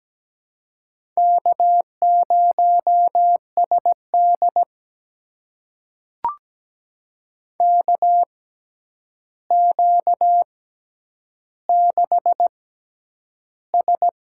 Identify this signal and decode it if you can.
Morse code, heard as a keyed tone, about 17 wpm, spaced as for 9 wpm, 700 Hz. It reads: K0SD KQ6S
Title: USA Call Signs - 4 Characters - 6x Character Spacing - 17wpm